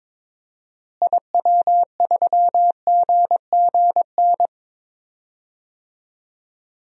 Morse code, heard as a keyed tone, about 22 wpm, 700 Hz. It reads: IW3GGN
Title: DXCC-100 Call Signs - 6 Characters - Repeated 2x - 22wpm